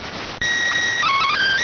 SSTV (slow-scan television), PD 290.